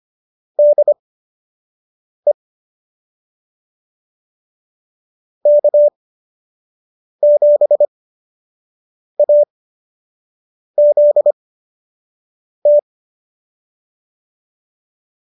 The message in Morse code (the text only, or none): DE K7AZT